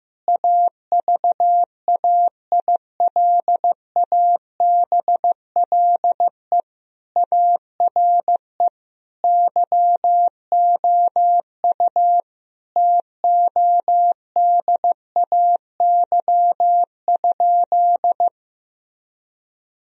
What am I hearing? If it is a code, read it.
Morse code, heard as a keyed tone, about 15 wpm, 700 Hz. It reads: AVAILABLE ARE YOU TODAY?
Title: Sentences from Top 1000 Words - Review Part 3 - 15wpm